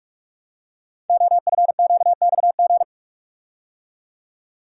Morse code, heard as a keyed tone, about 44 wpm, 700 Hz. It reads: OFQXG
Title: Code Groups - 5 Letters - 45wpm